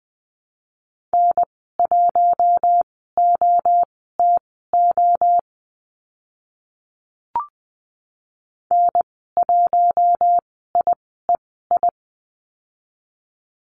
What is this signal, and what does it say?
Morse code, heard as a keyed tone, about 20 wpm, spaced as for 14 wpm, 700 Hz. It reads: N1OTO N1IEI
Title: 042 Lesson 8g - Number 1 - Call Signs 20wpm